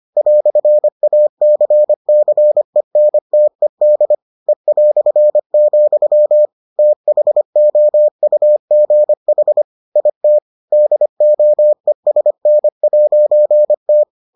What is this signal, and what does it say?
Morse code, heard as a keyed tone, about 25 wpm, 600 Hz. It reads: "ACCENTED E", THOUGH IT DOESN'T